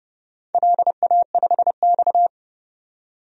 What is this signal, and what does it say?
Morse code, heard as a keyed tone, about 30 wpm, 700 Hz. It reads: LA5X
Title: DXCC-100 Call Signs - 4 Characters - 30wpm